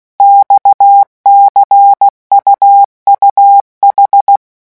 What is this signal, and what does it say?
Morse code, heard as a keyed tone, about 16 wpm, 800 Hz. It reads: XCUUH
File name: morse mp3 / revision5-vitesse 15 mots